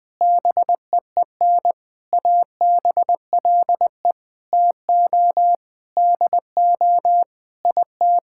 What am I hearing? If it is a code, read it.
Morse code, heard as a keyed tone, about 20 wpm, 700 Hz. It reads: BEEN ABLE TO DO IT
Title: Sentences with 6-Word Phrases - 20wpm